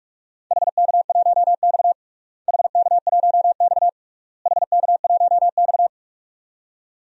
Morse code, heard as a keyed tone, about 46 wpm, 700 Hz. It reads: HK1X HK1X HK1X
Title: DXCC-100 Call Signs - 4 Characters - Repeated 3x - 45wpm